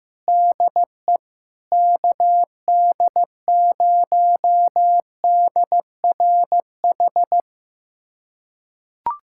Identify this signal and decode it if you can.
Morse code, heard as a keyed tone, about 15 wpm, 700 Hz. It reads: DE KD0DRH
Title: USA Callsign DE Callsign - 6 Characters - 15wpm